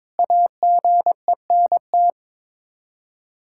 Morse code, heard as a keyed tone, about 22 wpm, 700 Hz. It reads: AGENT
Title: Practice QSO - Exchange 3 - More Info - 22wpm